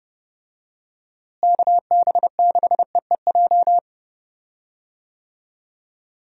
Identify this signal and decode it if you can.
Morse code, heard as a keyed tone, about 31 wpm, 700 Hz. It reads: KB6EEJ
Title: USA Call Signs - 6 Characters - Repeated 2x - 30wpm